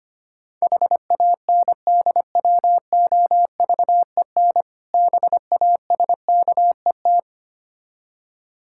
Morse code, heard as a keyed tone, about 25 wpm, 690 Hz. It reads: HANDWOVEN BASKET
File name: practice / 25wpm / Difficult Words with Hints - 25wpm